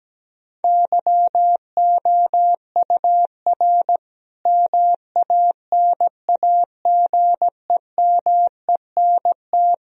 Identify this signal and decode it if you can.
Morse code, heard as a keyed tone, about 17 wpm, 700 Hz. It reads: YOUR MANAGEMENT
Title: Sentences from Top 1000 Words - Review Part 4 - 17wpm